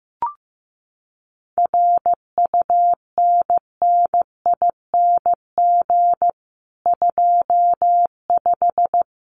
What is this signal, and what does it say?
Morse code, heard as a keyed tone, about 15 wpm, 700 Hz. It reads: RUNNING 25
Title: Common QSO Phrases 15wpm